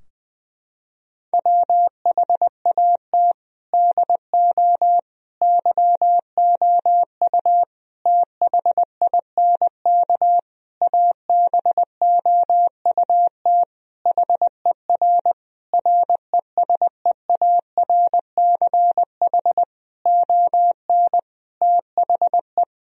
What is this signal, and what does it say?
Morse code, heard as a keyed tone, about 20 wpm, 700 Hz. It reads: WHAT DO YOU THINK ABOUT HER RESEARCH ON THE
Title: Sentences from Top 1000 Words - Review Part 4 - 20wpm